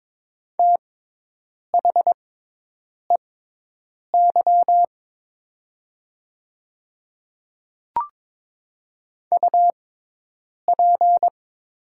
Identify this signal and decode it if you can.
Morse code, heard as a keyed tone, about 22 wpm, spaced as for 8 wpm, 700 Hz. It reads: THEY UP
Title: 6x Character Spacing - Top 500 Words - 22wpm